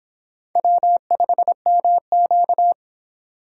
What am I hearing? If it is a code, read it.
Morse code, heard as a keyed tone, about 25 wpm, 700 Hz. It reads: W5MQ